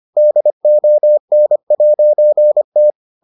Morse code, heard as a keyed tone, about 25 wpm, 600 Hz. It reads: DON'T